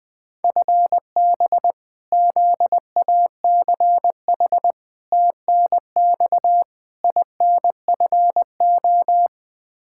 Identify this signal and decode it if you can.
Morse code, heard as a keyed tone, about 21 wpm, 700 Hz. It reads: FB ZACH TNX INFO
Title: K1USN SST QSO Practice 20wpm